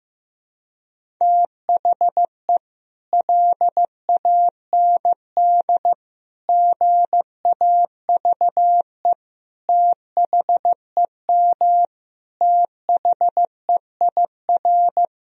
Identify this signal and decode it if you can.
Morse code, heard as a keyed tone, about 15 wpm, 700 Hz. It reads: THE LAND GAVE THEM THEIR
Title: Sentences from Top 1000 Words - Review Part 1 - 15wpm